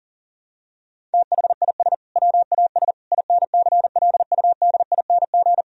Morse code, heard as a keyed tone, about 40 wpm, 700 Hz. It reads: THIS WAS INCLUDING
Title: Sentences from Top 1000 Words - Review Part 4 - 40wpm